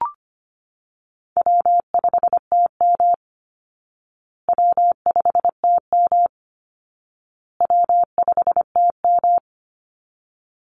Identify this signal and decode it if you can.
Morse code, heard as a keyed tone, about 25 wpm, 700 Hz. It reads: W5TM W5TM W5TM
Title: Call Signs Repeated 2x - 25wpm